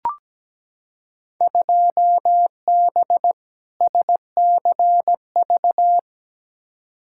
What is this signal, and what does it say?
Morse code, heard as a keyed tone, about 17 wpm, 700 Hz. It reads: 2B SCV
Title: Field Day Exchanges Encore - 17wpm